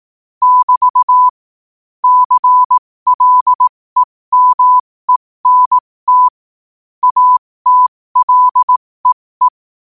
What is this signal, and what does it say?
Morse code, heard as a keyed tone, about 18 wpm, spaced as for 15 wpm, 1000 Hz. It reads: <BT> CLEMENT ATLEE
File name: qotdfiles / 15wpm